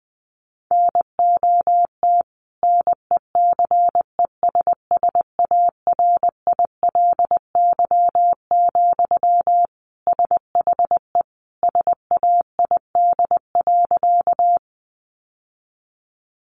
Morse code, heard as a keyed tone, about 20 wpm, 700 Hz. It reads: NOT NECESSARILY, SHE SAID.
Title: Sentences from Top 1400 Words 20wpm